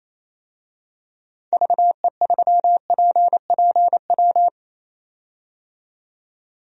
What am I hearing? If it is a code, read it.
Morse code, heard as a keyed tone, about 28 wpm, 700 Hz. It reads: VE3PPW
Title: DXCC-100 Call Signs - 6 Characters - 28wpm